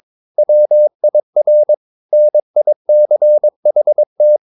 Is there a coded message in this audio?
Yes: Morse code from 0.4 to 4.4 s, heard as a keyed tone, about 22 wpm, 600 Hz. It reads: WIR NICHT